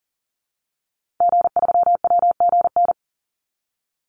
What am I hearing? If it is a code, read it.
Morse code, heard as a keyed tone, about 40 wpm, 700 Hz. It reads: G3WGN
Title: Call Signs - 40wpm